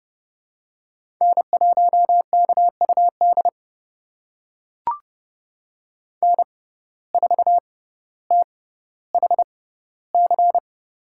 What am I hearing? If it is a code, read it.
Morse code, heard as a keyed tone, about 30 wpm, spaced as for 15 wpm, 700 Hz. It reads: N1KUD N4THC
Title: USA Call Signs - 5 Characters - 6x Character Spacing - 30wpm